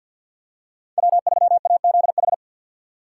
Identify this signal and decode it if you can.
Morse code, heard as a keyed tone, about 51 wpm, 700 Hz. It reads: W3AZH
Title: USA Call Signs - 5 Characters - 3x Character Spacing - 50wpm